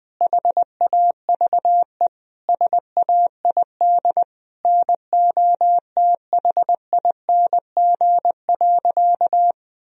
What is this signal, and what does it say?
Morse code, heard as a keyed tone, about 20 wpm, 700 Hz. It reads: HAVE SAID NOTHING.
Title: Sentences from Top 1000 Words - Review Part 4 - 20wpm